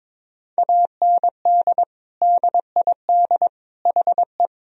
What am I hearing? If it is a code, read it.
Morse code, heard as a keyed tone, about 22 wpm, 700 Hz. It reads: AND DID HE
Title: Sentences from Top 1000 Words - Review Part 2 - 22wpm